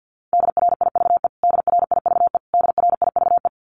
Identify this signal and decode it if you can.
Morse code, heard as a keyed tone, about 50 wpm, 700 Hz. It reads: DRIVE DRIVE DRIVE
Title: Top 500 Words Repeated 50wpm